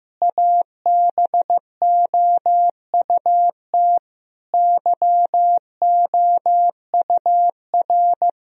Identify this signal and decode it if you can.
Morse code, heard as a keyed tone, about 14 wpm, 700 Hz. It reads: ABOUT YOUR
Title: Sentences from Top 1000 Words - Review Part 3 - 15wpm